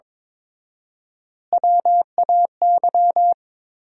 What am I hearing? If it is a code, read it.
Morse code, heard as a keyed tone, about 22 wpm, 700 Hz. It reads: WAY
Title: 4x Word-Spacing - Sentences from Top 100 Words - 22wpm